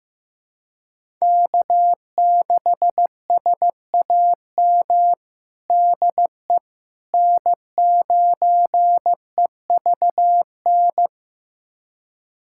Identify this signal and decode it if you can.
Morse code, heard as a keyed tone, about 15 wpm, 700 Hz. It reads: K6SAM DE N9EVN
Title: USA Callsign DE Callsign - 5 Characters - 15wpm